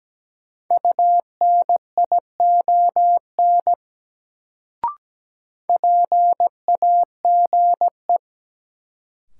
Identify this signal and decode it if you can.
Morse code, heard as a keyed tone, about 17 wpm, 700 Hz. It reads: UNION PAGE
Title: Sets of 5 Words Encore - New Word Drill - 17wpm